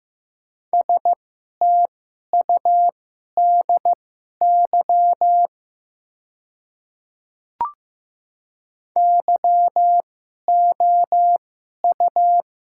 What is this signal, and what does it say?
Morse code, heard as a keyed tone, about 15 wpm, spaced as for 11 wpm, 700 Hz. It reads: STUDY YOU
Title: 2x Character Spacing - Top 300 Words - 15wpm